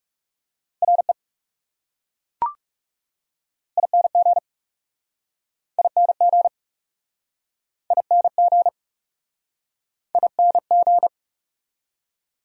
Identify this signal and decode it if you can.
Morse code, heard as a keyed tone, about 40 wpm, 700 Hz. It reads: RE ING ING ING ING